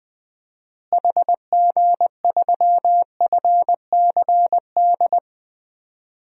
Morse code, heard as a keyed tone, about 20 wpm, 700 Hz. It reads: HG3FCD
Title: DXCC-100 Call Signs - 6 Characters - Repeated 2x - 20wpm